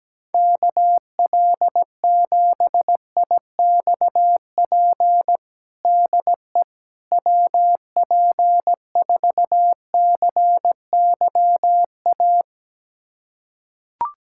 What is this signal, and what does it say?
Morse code, heard as a keyed tone, about 17 wpm, 700 Hz. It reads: KL7IXP DE WP4CYA
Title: USA Callsign DE Callsign - 6 Characters - 17wpm